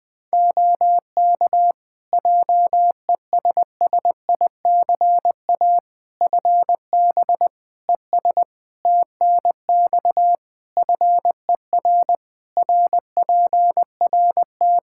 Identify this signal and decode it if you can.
Morse code, heard as a keyed tone, about 20 wpm, 700 Hz. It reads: OK JESSICA FB ES TNX FER RPRT
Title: Practice QSO - Exchange 2 - Info - 20wpm